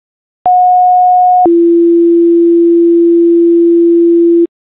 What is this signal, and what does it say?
Two-tone page, heard as a one-second tone, then a three-second tone, about 730 then 340 Hz. It is heard from 0.5 to 4.5 s.